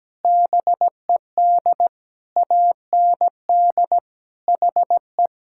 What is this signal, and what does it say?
Morse code, heard as a keyed tone, about 17 wpm, 700 Hz. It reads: BED AND HE
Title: Sentences from Top 1600 Words 17wpm